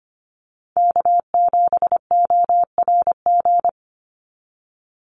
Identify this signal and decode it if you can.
Morse code, heard as a keyed tone, about 25 wpm, 700 Hz. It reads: K7ORG
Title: NA SOTA Chaser Callsigns 25wpm